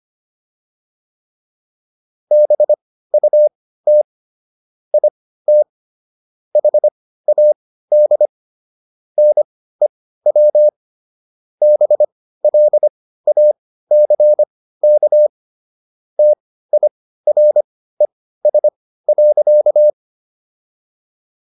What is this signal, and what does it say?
Morse code, heard as a keyed tone, about 25 wpm, spaced as for 15 wpm, 600 Hz. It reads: BUT IT HAD NEW BLACK TIRES.